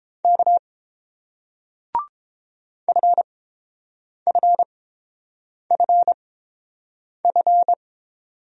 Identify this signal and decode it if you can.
Morse code, heard as a keyed tone, about 31 wpm, spaced as for 7 wpm, 690 Hz. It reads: K FFFF